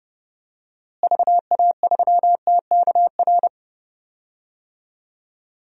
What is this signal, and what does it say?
Morse code, heard as a keyed tone, about 31 wpm, 700 Hz. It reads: VA3TKR
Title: DXCC-100 Call Signs - 6 Characters - Repeated 2x - 30wpm